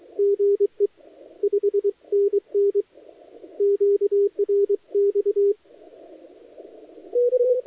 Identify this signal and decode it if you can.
Morse code, heard as a keyed tone, about 23 wpm, 400 Hz. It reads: GE 5NN QRX
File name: dlazaawansowanych / nagraniazpasma / titawaNETcontrol